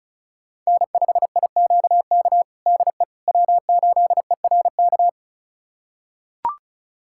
Morse code, heard as a keyed tone, about 36 wpm, 690 Hz. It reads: N5IQK DE W8ERK
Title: USA Callsign DE Callsign - 5 Characters - 35wpm